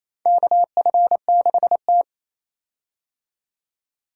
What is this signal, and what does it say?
Morse code, heard as a keyed tone, about 28 wpm, 700 Hz. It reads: KF6T
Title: K1USN SST QSO Practice - 28wpm